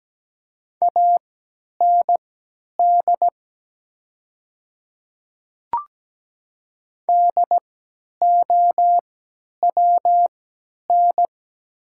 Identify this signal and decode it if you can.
Morse code, heard as a keyed tone, about 17 wpm, spaced as for 10 wpm, 700 Hz. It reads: AND DOWN
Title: 3x Character Spacing - Top 200 Words - 17wpm